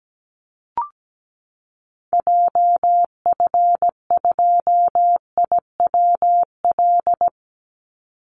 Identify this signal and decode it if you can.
Morse code, heard as a keyed tone, about 17 wpm, 700 Hz. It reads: JF2IWL
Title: Call Signs Repeated 3x - 17wpm